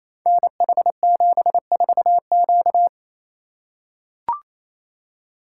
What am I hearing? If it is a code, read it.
Morse code, heard as a keyed tone, about 28 wpm, 700 Hz. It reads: NH74Q
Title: Code Groups - 5 Letter:Numbers - 28wpm